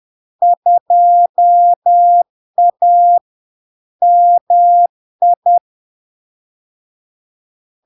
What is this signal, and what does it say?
Morse code, heard as a keyed tone, about 10 wpm, 700 Hz. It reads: 2A MI
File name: LARK-FD-2022-Log-in-CW-10wpm-700-Hz